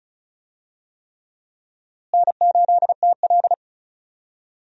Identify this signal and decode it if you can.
Morse code, heard as a keyed tone, about 34 wpm, 700 Hz. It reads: N8TL
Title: USA Call Signs - 4 Characters - Repeated 3x - 35wpm